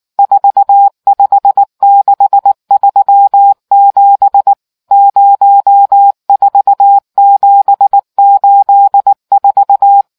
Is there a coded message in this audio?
Yes: Morse code from 0.2 to 10.0 s, heard as a keyed tone, about 19 wpm, 800 Hz. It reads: 45637 04784